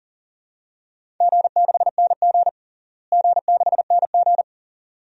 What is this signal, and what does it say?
Morse code, heard as a keyed tone, about 40 wpm, 700 Hz. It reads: G6NG G6NG
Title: DXCC-100 Call Signs - 4 Characters - Repeated 2x - 40wpm